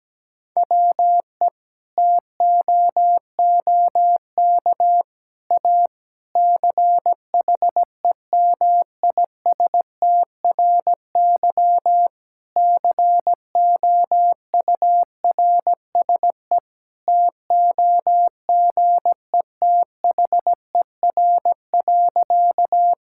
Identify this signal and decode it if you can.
Morse code, heard as a keyed tone, about 17 wpm, 700 Hz. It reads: WE TOOK A CHEMISTRY COURSE TOGETHER.
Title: Sentences from Top 1000 Words - Review Part 6 - 17wpm